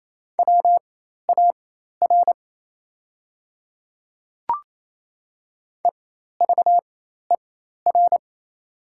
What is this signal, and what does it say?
Morse code, heard as a keyed tone, about 28 wpm, spaced as for 13 wpm, 700 Hz. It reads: WAR EVER
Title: 4x Character Spacing - Top 500 Words - 28wpm